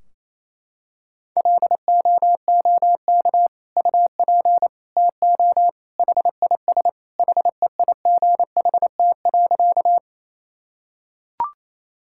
Morse code, heard as a keyed tone, about 28 wpm, 700 Hz. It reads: LOOK UP TO HIS HEIGHT.
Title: Sentences from Top 1600 Words 28wpm